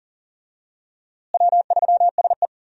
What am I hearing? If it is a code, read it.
Morse code, heard as a keyed tone, about 40 wpm, 700 Hz. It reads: W3SE
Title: USA Call Signs - 4 Characters - Repeated 2x - 40wpm